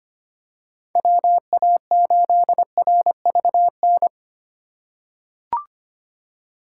Morse code, heard as a keyed tone, about 25 wpm, 700 Hz. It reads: WA8RVN
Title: USA Call Signs - 2x Character Spacing - 25wpm